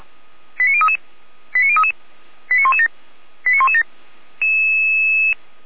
Five-tone selective call (ZVEI), heard as quick runs of five tones, tones of about 70 ms. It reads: 89033 89033 89117 89117